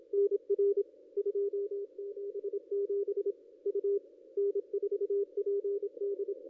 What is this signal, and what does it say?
Morse code, heard as a keyed tone, about 26 wpm, 440 Hz, with radio noise. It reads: NR 277 U N4PD
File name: contests / ss-cw-k5tr-op-n5rz -0500